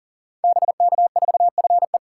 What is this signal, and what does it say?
Morse code, heard as a keyed tone, about 42 wpm, 700 Hz. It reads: BK4FE
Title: Code Groups - 5 Letter:Numbers - 40wpm